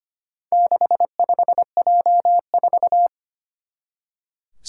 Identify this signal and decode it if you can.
Morse code, heard as a keyed tone, about 25 wpm, 700 Hz. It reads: 65J4
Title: Four Challenging Letter-Number Combinations 25wpm